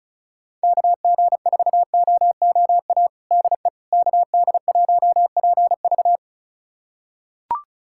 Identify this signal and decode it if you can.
Morse code, heard as a keyed tone, about 36 wpm, 700 Hz. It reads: KG4OOA DE KD1PV
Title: USA Callsign DE Callsign 35wpm